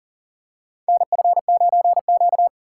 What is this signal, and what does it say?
Morse code, heard as a keyed tone, about 37 wpm, 700 Hz. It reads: NF9Q
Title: USA Call Signs - 4 Characters - 3x Character Spacing - 40wpm